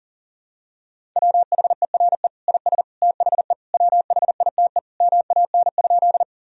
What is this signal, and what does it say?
Morse code, heard as a keyed tone, about 40 wpm, 700 Hz. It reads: WHERE IS THE WHITE MAN?
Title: Sentences from Top 1000 Words - Review Part 1 - 40wpm